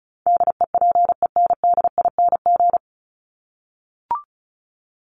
Morse code, heard as a keyed tone, about 34 wpm, 700 Hz. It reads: DEPENDING
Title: ING Suffix - Top 100 Words with Base Words 35wpm